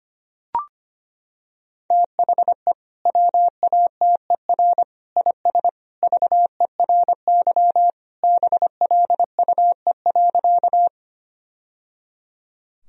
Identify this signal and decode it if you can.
Morse code, heard as a keyed tone, about 25 wpm, 700 Hz. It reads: THE WATER IS VERY BLUE.
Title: Sentences from Top 1000 Words - Review Part 3 - 25wpm